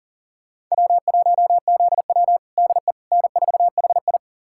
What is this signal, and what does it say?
Morse code, heard as a keyed tone, about 40 wpm, 700 Hz. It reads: W1ZW DE N4HI
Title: USA Callsign DE Callsign - 4 Characters - 40wpm